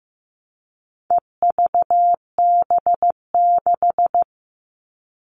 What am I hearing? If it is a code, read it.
Morse code, heard as a keyed tone, about 15 wpm, 700 Hz. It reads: EVB6
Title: All Cut Numbers - 4-Digit 15wpm